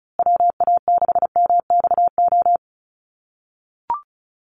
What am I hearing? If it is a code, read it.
Morse code, heard as a keyed tone, about 34 wpm, 700 Hz. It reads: WA6MXO
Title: USA Call Signs Repeated 2x 35wpm